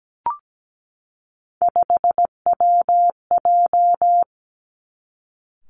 Morse code, heard as a keyed tone, about 17 wpm, 700 Hz. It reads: 5WJ
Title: Three Challenging Letter-Number Combinations 17wpm